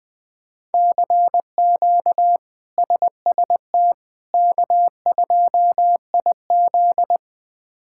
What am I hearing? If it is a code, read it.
Morse code, heard as a keyed tone, about 20 wpm, 700 Hz. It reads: CQ SST K2IZ
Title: K1USN SST QSO Practice - 20wpm